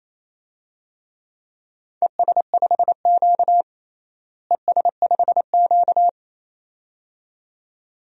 Morse code, heard as a keyed tone, about 28 wpm, 700 Hz. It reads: ES5Q ES5Q
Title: DXCC-100 Call Signs - 4 Characters - Repeated 2x - 28wpm